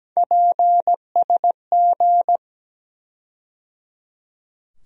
Morse code, heard as a keyed tone, about 17 wpm, 700 Hz. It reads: PSG
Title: USA Call Sign Postfixes - 3 Letters 17wpm